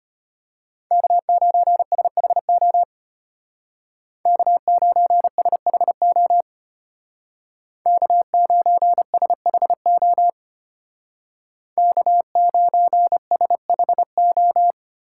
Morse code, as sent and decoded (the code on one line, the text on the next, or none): -.- ----. ... .... --- / -.- ----. ... .... --- / -.- ----. ... .... --- / -.- ----. ... .... ---
K9SHO K9SHO K9SHO K9SHO